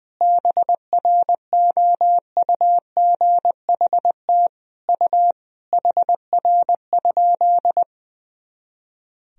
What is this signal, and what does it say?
Morse code, heard as a keyed tone, about 19 wpm, 700 Hz. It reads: BROUGHT U HR?
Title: Sentences from Top 500 Words and Abbreviations 20wpm